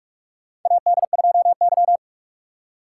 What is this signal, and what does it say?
Morse code, heard as a keyed tone, about 45 wpm, 700 Hz. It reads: AD2Y
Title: USA Call Signs - 4 Characters - 2x Character Spacing - 45wpm